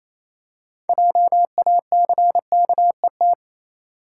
Morse code, heard as a keyed tone, about 27 wpm, 700 Hz. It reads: JACKET
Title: Six Letter Words - 28wpm